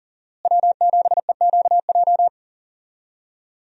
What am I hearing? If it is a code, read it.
Morse code, heard as a keyed tone, about 40 wpm, 700 Hz. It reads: W7EQJ
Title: USA Call Signs - 5 Characters - Repeated 2x - 40wpm